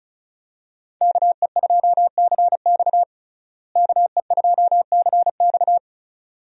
Morse code, heard as a keyed tone, about 36 wpm, 700 Hz. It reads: KE2CX KE2CX
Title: USA Call Signs - 5 Characters - Repeated 2x - 35wpm